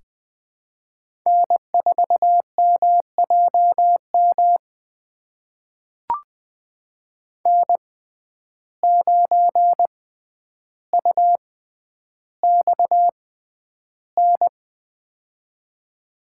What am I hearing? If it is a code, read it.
Morse code, heard as a keyed tone, about 20 wpm, spaced as for 10 wpm, 700 Hz. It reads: N4MJM N9UXN
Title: USA Call Signs - 6x Character Spacing - 20wpm